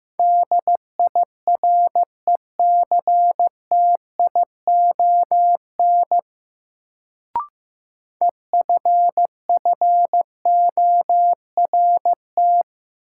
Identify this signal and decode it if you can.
Morse code, heard as a keyed tone, about 15 wpm, 700 Hz. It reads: DIRECTION EFFORT
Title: Sets of 4 Words - New Word Drill - 15wpm